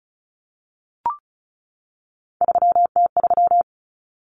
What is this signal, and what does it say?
Morse code, heard as a keyed tone, about 34 wpm, 700 Hz. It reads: 3T3
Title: Common Cut Numbers - 3-Digit 35wpm